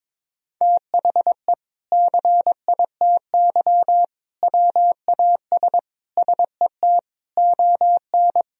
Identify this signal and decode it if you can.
Morse code, heard as a keyed tone, about 22 wpm, 700 Hz. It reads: THE CITY WAS SET ON
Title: Sentences from Top 1000 Words - Review Part 1 - 22wpm